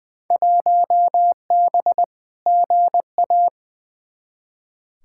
Morse code, heard as a keyed tone, about 20 wpm, 700 Hz. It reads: 1B GA
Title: Field Day Exchanges Encore - 20wpm